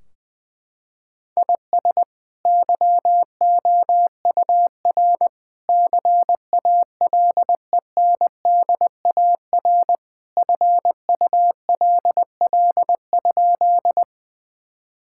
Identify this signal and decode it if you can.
Morse code, heard as a keyed tone, about 21 wpm, 700 Hz. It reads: IS YOUR CALENDAR FULL?
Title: Sentences from Top 1600 Words 20wpm